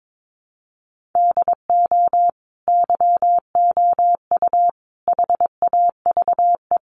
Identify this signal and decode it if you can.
Morse code, heard as a keyed tone, about 22 wpm, 700 Hz. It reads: DO YOU HAVE
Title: Sentences from Top 600 Words 22wpm